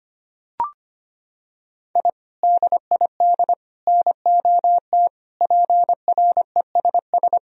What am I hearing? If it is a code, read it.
Morse code, heard as a keyed tone, about 25 wpm, 700 Hz. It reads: I DID NOT PRESS
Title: Sentences from Top 1000 Words - Review Part 1 - 25wpm